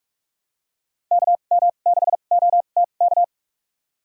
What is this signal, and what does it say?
Morse code, heard as a keyed tone, about 41 wpm, spaced as for 31 wpm, 700 Hz. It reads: KM6OTK
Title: USA Call Signs - 2x Character Spacing - 45wpm